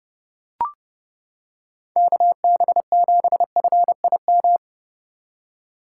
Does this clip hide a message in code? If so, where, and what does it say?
Yes: Morse code from 2.0 to 4.6 s, heard as a keyed tone, about 30 wpm, 700 Hz. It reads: KB7FIM